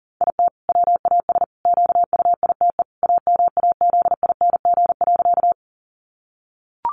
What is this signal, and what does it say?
Morse code, heard as a keyed tone, about 37 wpm, 700 Hz. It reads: IT WAS QUITE AMAZING.
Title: Sentences from Top 1500 Words 40wpm